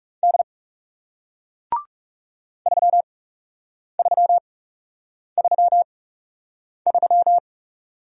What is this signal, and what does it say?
Morse code, heard as a keyed tone, about 40 wpm, spaced as for 8 wpm, 700 Hz. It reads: D 3333